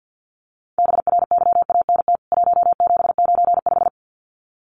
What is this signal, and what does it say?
Morse code, heard as a keyed tone, about 50 wpm, 700 Hz. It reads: BRYANT 1795
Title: CWT Exchange 50wpm